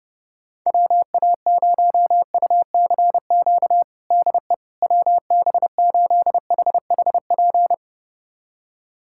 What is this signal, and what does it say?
Morse code, heard as a keyed tone, about 32 wpm, 700 Hz. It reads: WA0UCQ DE WB8HHP
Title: USA Callsign DE Callsign - 6 Characters - 30wpm